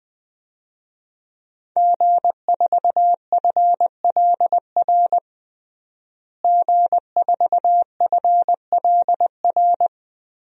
Morse code, heard as a keyed tone, about 20 wpm, 700 Hz. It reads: G4FLR G4FLR
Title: DXCC-100 Call Signs - 5 Characters - Repeated 3x - 20wpm